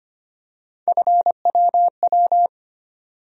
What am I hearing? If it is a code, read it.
Morse code, heard as a keyed tone, about 25 wpm, 700 Hz. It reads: FWW
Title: USA Call Sign Postfixes - 3 Letters 25wpm